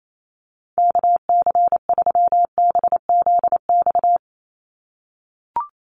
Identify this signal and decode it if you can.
Morse code, heard as a keyed tone, about 28 wpm, 700 Hz. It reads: KC3BZX
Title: USA Call Signs 28wpm